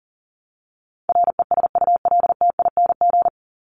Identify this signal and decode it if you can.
Morse code, heard as a keyed tone, about 40 wpm, 700 Hz. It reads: RESULTING
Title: ING Suffix - Top 100 Words with Base Words 40wpm